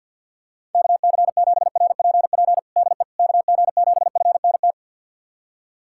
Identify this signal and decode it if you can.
Morse code, heard as a keyed tone, about 46 wpm, 700 Hz. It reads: KC7RPP DE KG7FNT
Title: USA Callsign DE Callsign - 6 Characters - 50wpm